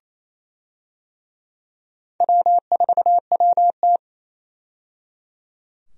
Morse code, heard as a keyed tone, about 27 wpm, 700 Hz. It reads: W4WT
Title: USA Call Signs - 4 Characters - 28wpm